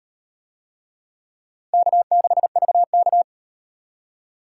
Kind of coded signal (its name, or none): Morse code